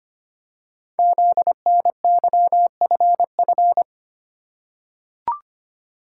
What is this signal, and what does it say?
Morse code, heard as a keyed tone, about 25 wpm, 700 Hz. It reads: ZNYFF
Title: Code Groups - 5 Letters - 25wpm